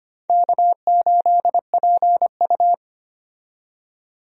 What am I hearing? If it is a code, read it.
Morse code, heard as a keyed tone, about 25 wpm, 700 Hz. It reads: K8PU
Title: USA Call Signs - 4 Characters - 25wpm